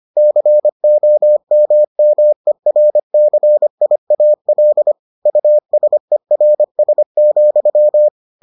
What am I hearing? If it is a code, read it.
Morse code, heard as a keyed tone, about 25 wpm, 600 Hz. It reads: COMMERCIAL USERS,